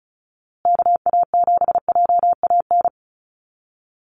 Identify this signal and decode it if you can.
Morse code, heard as a keyed tone, about 34 wpm, 700 Hz. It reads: KA7JAN
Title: USA Call Signs Repeated 2x 35wpm